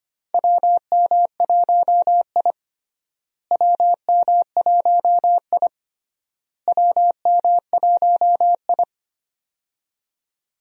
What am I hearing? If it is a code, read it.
Morse code, heard as a keyed tone, about 25 wpm, 700 Hz. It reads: WM1I WM1I WM1I
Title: USA Call Signs - 4 Characters - Repeated 3x - 25wpm